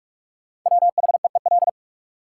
Morse code, heard as a keyed tone, about 45 wpm, 700 Hz. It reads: WHEEL
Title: Sentences from Top 500 Words - New Word Drill - 45wpm